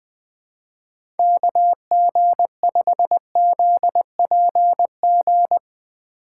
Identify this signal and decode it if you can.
Morse code, heard as a keyed tone, about 20 wpm, 700 Hz. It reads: KG5ZPG
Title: Speed Racing - USA 2x3 Call Signs 17wpm